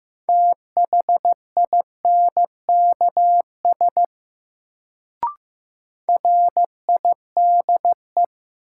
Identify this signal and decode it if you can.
Morse code, heard as a keyed tone, about 15 wpm, 700 Hz. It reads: THINKS RIDE
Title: Sentences from Top 1000 Words - Review Part 6 - New Word Drill - 15wpm